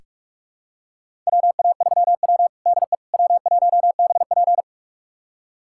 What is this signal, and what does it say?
Morse code, heard as a keyed tone, about 42 wpm, 700 Hz. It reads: WA3W DE W1BP
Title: USA Callsign DE Callsign - 4 Characters - 45wpm